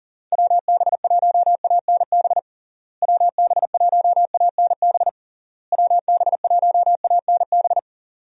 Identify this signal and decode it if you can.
Morse code, heard as a keyed tone, about 40 wpm, 700 Hz. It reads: WB1ANB WB1ANB WB1ANB
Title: USA Call Signs - 6 Characters - Repeated 3x - 40wpm